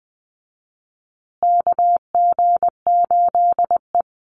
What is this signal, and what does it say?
Morse code, heard as a keyed tone, about 20 wpm, 700 Hz. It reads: KG8E
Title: USA Call Signs Repeated 3x 20wpm